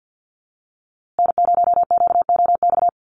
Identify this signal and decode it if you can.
Morse code, heard as a keyed tone, about 50 wpm, 700 Hz. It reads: N0QOX
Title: USA Call Signs Repeated 3x 50wpm